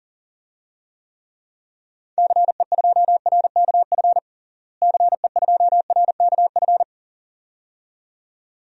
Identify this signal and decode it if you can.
Morse code, heard as a keyed tone, about 40 wpm, 700 Hz. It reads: CE2RKF CE2RKF
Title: DXCC-100 Call Signs - 6 Characters - Repeated 2x - 40wpm